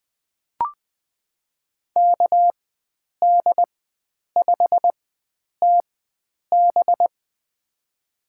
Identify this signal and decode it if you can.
Morse code, heard as a keyed tone, about 20 wpm, spaced as for 9 wpm, 700 Hz. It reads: KD5TB
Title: USA Call Signs - 5 Characters - 4x Character Spacing - 20wpm